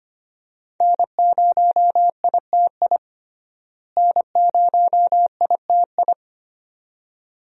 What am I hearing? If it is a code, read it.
Morse code, heard as a keyed tone, about 25 wpm, 700 Hz. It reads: N0ITI N0ITI
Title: USA Call Signs - 5 Characters - Repeated 2x - 25wpm